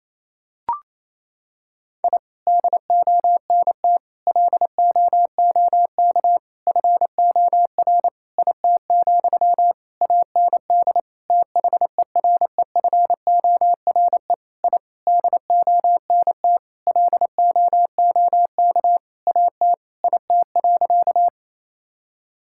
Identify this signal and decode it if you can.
Morse code, heard as a keyed tone, about 28 wpm, 700 Hz. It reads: I DONT LOOK FOR IT, AND THEREFORE I DONT LOOK AT IT.
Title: Sentences from Top 1000 Words - Review Part 5 - 28wpm